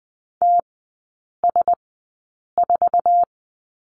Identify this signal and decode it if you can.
Morse code, heard as a keyed tone, about 20 wpm, spaced as for 8 wpm, 700 Hz. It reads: TS4
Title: 053 Lesson 10e - Letter r - Three Characters 20wpm